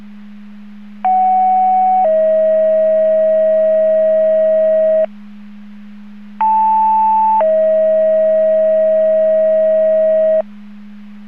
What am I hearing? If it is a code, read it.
Two-tone page, heard as a one-second tone, then a three-second tone, about 750 then 640 Hz, about 880 then 640 Hz. It is heard from 1.0 to 10.4 s.